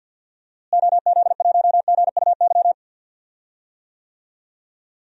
Morse code, heard as a keyed tone, about 51 wpm, 700 Hz. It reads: OZ1GUY
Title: DXCC-100 Call Signs - 6 Characters - Repeated 3x - 50wpm